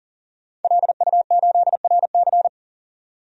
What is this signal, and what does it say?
Morse code, heard as a keyed tone, about 40 wpm, 700 Hz. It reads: LU8RC
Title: DXCC-100 Call Signs - 5 Characters - 40wpm